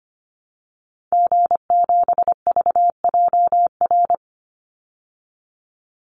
Morse code, heard as a keyed tone, about 25 wpm, 700 Hz. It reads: G7VJR
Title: Call Signs Repeated 2x - 25wpm